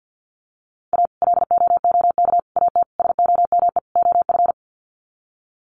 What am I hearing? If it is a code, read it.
Morse code, heard as a keyed tone, about 50 wpm, 700 Hz. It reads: A LOOK AT SOME OF
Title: Sets of 5 Words Encore2 50wpm